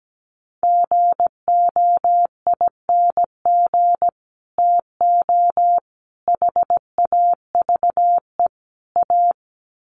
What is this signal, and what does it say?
Morse code, heard as a keyed tone, about 17 wpm, 700 Hz. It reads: GOING TO HAVE A
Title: Sets of 4 Words 17wpm